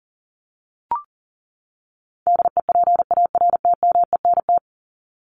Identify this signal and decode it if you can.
Morse code, heard as a keyed tone, about 40 wpm, 700 Hz. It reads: DEPARTMENT
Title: Top 751-1000 Words 40wpm